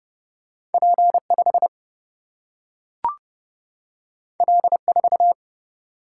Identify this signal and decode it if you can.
Morse code, heard as a keyed tone, about 30 wpm, 700 Hz. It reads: P5 L4
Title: Single-Letter Single-Number 30wpm